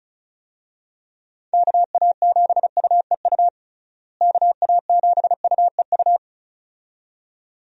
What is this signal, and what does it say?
Morse code, heard as a keyed tone, about 34 wpm, 700 Hz. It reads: KA7UEU KA7UEU
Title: USA Call Signs - 6 Characters - Repeated 2x - 35wpm